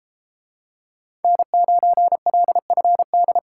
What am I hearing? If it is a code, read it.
Morse code, heard as a keyed tone, about 30 wpm, 700 Hz. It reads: N9LFD